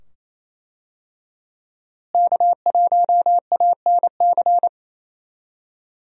Morse code, heard as a keyed tone, about 28 wpm, 700 Hz. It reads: K1ANC
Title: USA Call Signs - 5 Characters - Repeated 2x - 28wpm